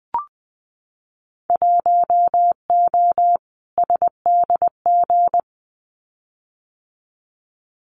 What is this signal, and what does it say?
Morse code, heard as a keyed tone, about 20 wpm, 700 Hz. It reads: 1O SDG
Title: Winter Field Day Exchanges Encore 20wpm